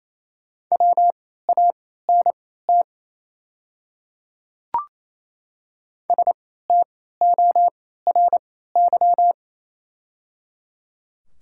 Morse code, heard as a keyed tone, about 28 wpm, spaced as for 16 wpm, 700 Hz. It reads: WANT STORY
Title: 3x Character Spacing - Top 300 Words - 28wpm